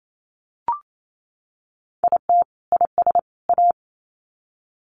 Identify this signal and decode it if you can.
Morse code, heard as a keyed tone, about 28 wpm, 700 Hz. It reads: IT IS A
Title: Sets of 3 Words 28wpm